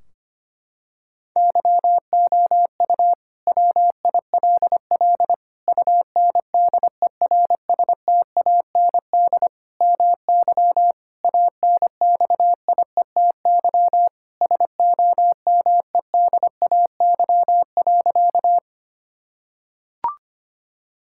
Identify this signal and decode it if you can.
Morse code, heard as a keyed tone, about 25 wpm, 700 Hz. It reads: YOU WILL UNDERSTAND MY ANXIETY SOMEDAY.
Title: Sentences from Top 1600 Words 25wpm